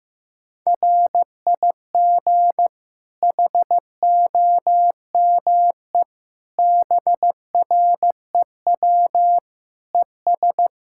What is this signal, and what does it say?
Morse code, heard as a keyed tone, about 15 wpm, 700 Hz. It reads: RIG HOME BREW ES
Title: Practice QSO - Exchange 2 - Info - 15wpm